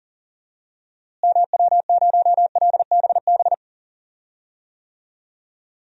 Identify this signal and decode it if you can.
Morse code, heard as a keyed tone, about 40 wpm, 700 Hz. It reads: MW0LBB
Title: DXCC-100 Call Signs - 6 Characters - Repeated 2x - 40wpm